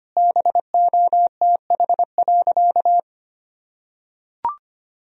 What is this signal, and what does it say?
Morse code, heard as a keyed tone, about 25 wpm, 700 Hz. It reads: BOTH.
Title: Sentences from Top 1000 Words - Review Part 1 - 25wpm